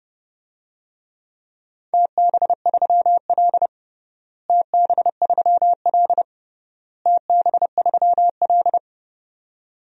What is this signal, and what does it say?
Morse code, heard as a keyed tone, about 30 wpm, 700 Hz. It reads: TB3L TB3L TB3L
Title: DXCC-100 Call Signs - 4 Characters - Repeated 3x - 30wpm